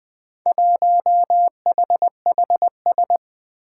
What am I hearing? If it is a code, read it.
Morse code, heard as a keyed tone, about 20 wpm, 700 Hz. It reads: 1HHS
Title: Four Challenging Letter-Number Combinations 20wpm